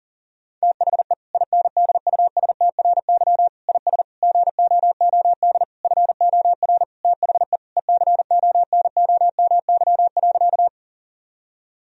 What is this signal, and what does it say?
Morse code, heard as a keyed tone, about 40 wpm, 700 Hz. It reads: THE INDUSTRY IS GOOD FOR THE ECONOMY.
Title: Sentences from Top 1000 Words - Review Part 4 - 40wpm